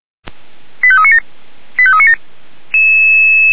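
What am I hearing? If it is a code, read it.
Five-tone selective call (ZVEI), heard as quick runs of five tones, tones of about 80 ms. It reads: 85397 85397
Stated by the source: ZVEI five-tone selective call received on an SDR.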